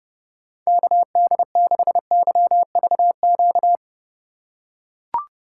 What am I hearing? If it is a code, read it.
Morse code, heard as a keyed tone, about 32 wpm, 700 Hz. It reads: KD6YVQ
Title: USA Call Signs - 6 Characters - 4x Character Spacing - 30wpm